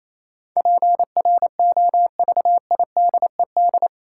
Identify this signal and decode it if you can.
Morse code, heard as a keyed tone, about 28 wpm, 700 Hz. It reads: PROVIDED
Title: Sentences from Top 1500 Words - New Word Drill - 28wpm